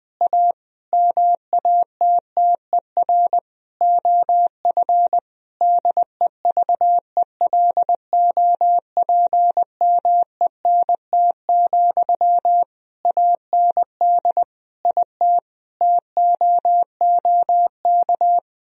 Morse code, heard as a keyed tone, about 20 wpm, 700 Hz. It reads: A MATTER OF DEVELOPMENT, AND IT TOOK
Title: Sentences from Top 1000 Words - Review Part 4 - 20wpm